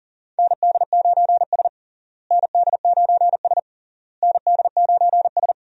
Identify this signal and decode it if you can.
Morse code, heard as a keyed tone, about 40 wpm, 700 Hz. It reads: ND9S ND9S ND9S
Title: USA Call Signs - 4 Characters - Repeated 3x - 40wpm